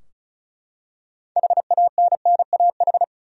Morse code, heard as a keyed tone, about 34 wpm, 700 Hz. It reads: HANNAH
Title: USA Common Names - 35wpm